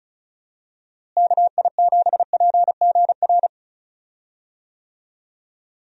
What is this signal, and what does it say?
Morse code, heard as a keyed tone, about 34 wpm, 700 Hz. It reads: KI7PGR
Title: USA Call Signs - 6 Characters - Repeated 3x - 35wpm